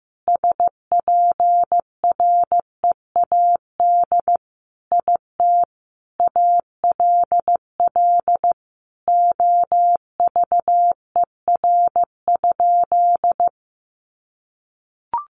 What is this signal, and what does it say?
Morse code, heard as a keyed tone, about 15 wpm, 700 Hz. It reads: SPREAD IT ALL OVER?
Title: Sentences from Top 1200 Words 15wpm